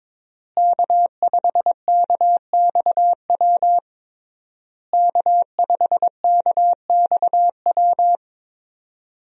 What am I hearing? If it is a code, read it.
Morse code, heard as a keyed tone, about 21 wpm, 700 Hz. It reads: K5KXW K5KXW
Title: USA Call Signs - 5 Characters - Repeated 3x - 22wpm